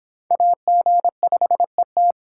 Morse code, heard as a keyed tone, about 26 wpm, 700 Hz. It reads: AG5ET